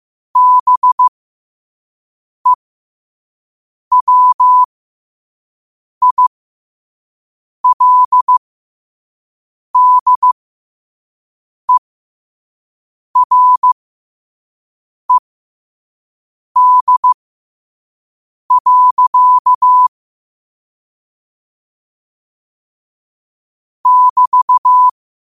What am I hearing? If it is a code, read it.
Morse code, heard as a keyed tone, about 15 wpm, spaced as for 5 wpm, 1000 Hz. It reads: BEWILDERED. <BT>
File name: qotdfiles / 5wpm